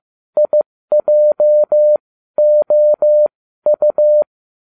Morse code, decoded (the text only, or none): IJOU